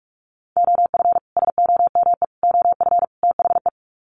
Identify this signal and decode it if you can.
Morse code, heard as a keyed tone, about 44 wpm, 700 Hz. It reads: OF SOME OF THE
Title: Sets of 4 Words 45wpm